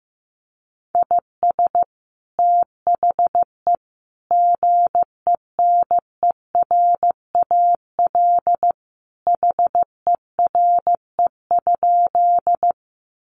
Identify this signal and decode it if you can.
Morse code, heard as a keyed tone, about 15 wpm, 700 Hz. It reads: IS THE GENERAL HERE?
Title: Sentences from Top 600 Words 15wpm